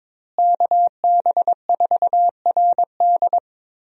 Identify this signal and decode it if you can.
Morse code, heard as a keyed tone, about 22 wpm, 700 Hz. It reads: KB4RD
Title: USA Call Signs - 5 Characters - Repeated 2x - 22wpm